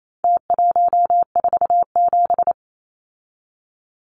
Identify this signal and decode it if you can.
Morse code, heard as a keyed tone, about 28 wpm, 700 Hz. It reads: T147
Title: Numbers - Contest Style - 4-Digit 28wpm